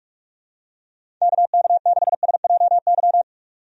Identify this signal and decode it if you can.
Morse code, heard as a keyed tone, about 42 wpm, 700 Hz. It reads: KK6SJY
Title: USA Call Signs - 6 Characters - 45wpm